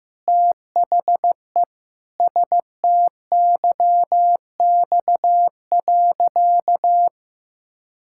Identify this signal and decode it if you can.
Morse code, heard as a keyed tone, about 15 wpm, 700 Hz. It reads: THE STYX.
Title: Pangram Challenge v2 - 15wpm